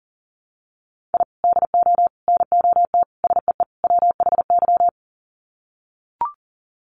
Morse code, heard as a keyed tone, about 37 wpm, 700 Hz. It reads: I DO NOT SEE WHY
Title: Sets of 5 Words Encore 40wpm